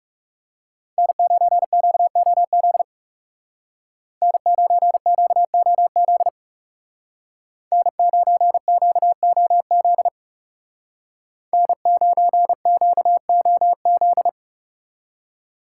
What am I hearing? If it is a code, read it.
Morse code, heard as a keyed tone, about 36 wpm, 700 Hz. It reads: N9QOZ N9QOZ N9QOZ N9QOZ
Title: Speed Racing - USA Call Signs 30wpm